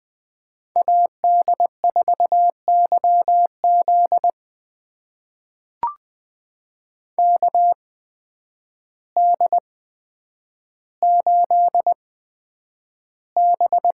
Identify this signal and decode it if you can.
Morse code, heard as a keyed tone, about 20 wpm, spaced as for 9 wpm, 700 Hz. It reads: AD4YZ KD8B
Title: USA Call Signs - 5 Characters - 8x Character Spacing - 20wpm